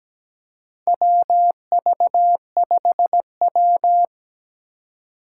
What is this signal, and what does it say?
Morse code, heard as a keyed tone, about 17 wpm, 700 Hz. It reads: WV5W
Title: USA Call Signs - 3x Character Spacing - 17wpm